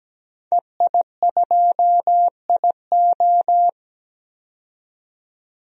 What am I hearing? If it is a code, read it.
Morse code, heard as a keyed tone, about 17 wpm, 700 Hz. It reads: EI2IO
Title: DXCC-100 Call Signs - 5 Characters - 17wpm